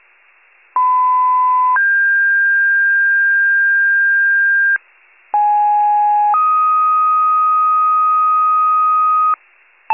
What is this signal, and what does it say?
Two-tone page, heard as a one-second tone, then a three-second tone, about 1010 then 1600 Hz, about 830 then 1250 Hz, with noise. It is heard from 0.8 to 9.3 s.
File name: wcalert